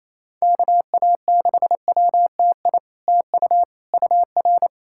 Morse code, heard as a keyed tone, about 28 wpm, 700 Hz. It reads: KA6WTI TU UR
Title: SOTA QSO Practice 28wpm